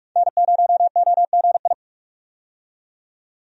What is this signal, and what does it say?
Morse code, heard as a keyed tone, about 41 wpm, 700 Hz. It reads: N0OGI
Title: USA Call Signs - 5 Characters - 45wpm